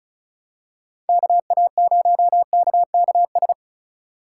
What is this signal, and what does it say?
Morse code, heard as a keyed tone, about 36 wpm, 700 Hz. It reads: KA0KKS
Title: USA Call Signs - 6 Characters - 3x Character Spacing - 35wpm